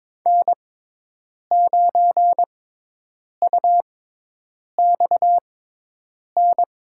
Morse code, heard as a keyed tone, about 22 wpm, spaced as for 8 wpm, 700 Hz. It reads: N9UXN
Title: USA Call Signs - 5 Characters - 6x Character Spacing - 22wpm